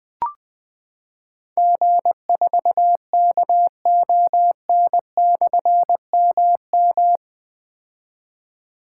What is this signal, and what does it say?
Morse code, heard as a keyed tone, about 20 wpm, 700 Hz. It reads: G4KON/MM
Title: Difficult Call Signs 20wpm